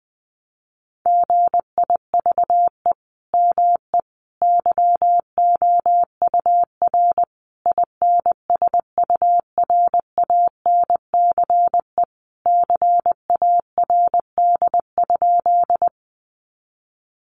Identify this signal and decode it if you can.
Morse code, heard as a keyed tone, about 19 wpm, 700 Hz. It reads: GIVE ME YOUR INSURANCE CARD?
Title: Sentences from Top 1100 Words 20wpm